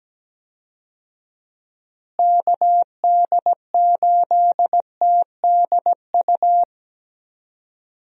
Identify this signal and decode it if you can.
Morse code, heard as a keyed tone, about 17 wpm, 700 Hz. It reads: KD8TDU
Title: USA Call Signs - 6 Characters - Repeated 3x - 17wpm